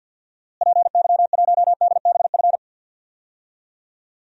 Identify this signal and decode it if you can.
Morse code, heard as a keyed tone, about 50 wpm, 700 Hz. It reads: PY1DBF
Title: DXCC-100 Call Signs - 6 Characters - Repeated 2x - 50wpm